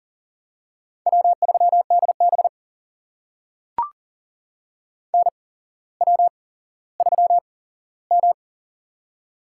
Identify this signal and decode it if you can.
Morse code, heard as a keyed tone, about 42 wpm, spaced as for 17 wpm, 700 Hz. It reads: W3DB NW3M
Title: USA Call Signs - 4 Characters - 8x Character Spacing - 40wpm